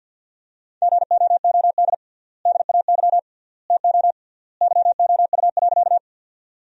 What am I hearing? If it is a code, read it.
Morse code, heard as a keyed tone, about 50 wpm, 700 Hz. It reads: GOOD DAY TO YOU.
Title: 3x Word-Spacing - Sentences from Top 200 Words - 50wpm